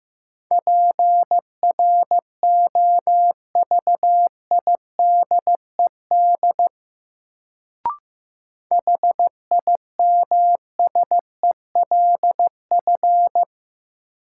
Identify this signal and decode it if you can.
Morse code, heard as a keyed tone, about 15 wpm, 700 Hz. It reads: PROVIDED HIMSELF
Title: Sentences from Top 1500 Words - New Word Drill Alt - 15wpm